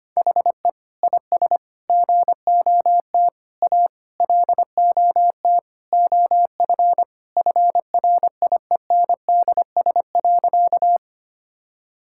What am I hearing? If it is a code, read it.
Morse code, heard as a keyed tone, about 25 wpm, 700 Hz. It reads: HE IS GOT A LOT OF FRIENDS.
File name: practice / 25wpm / Sentences with 4-Word Phrases - 25wpm